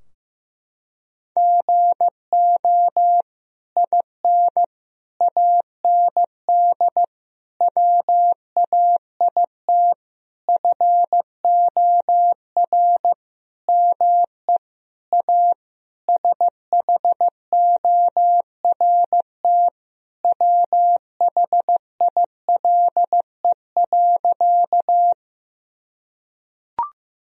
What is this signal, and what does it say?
Morse code, heard as a keyed tone, about 16 wpm, 700 Hz. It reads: GO IN AND WAIT FOR ME A SHORT WHILE.
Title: Sentences from Top 1000 Words - Review Part 2 - 15wpm